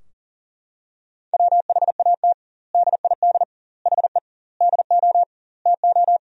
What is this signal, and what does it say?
Morse code, heard as a keyed tone, about 40 wpm, 700 Hz. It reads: WHAT DID HE DO TO
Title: 2x Word-Spacing - Sentences from Top 100 Words - 40wpm